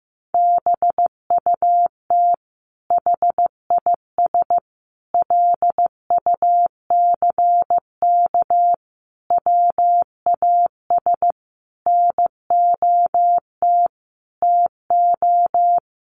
Morse code, heard as a keyed tone, about 15 wpm, 700 Hz. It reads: BUT HIS LUCK WAS NOT TO
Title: Sentences from Top 1500 Words 15wpm